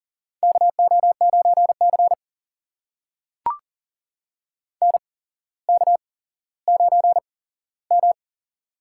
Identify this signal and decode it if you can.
Morse code, heard as a keyed tone, about 40 wpm, spaced as for 11 wpm, 700 Hz. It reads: KO9CN K9M
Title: USA Call Signs - 8x Character Spacing - 40wpm